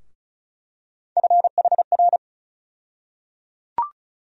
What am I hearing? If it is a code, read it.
Morse code, heard as a keyed tone, about 36 wpm, 700 Hz. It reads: FHR
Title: USA Call Sign Postfixes - 3 Letters 35wpm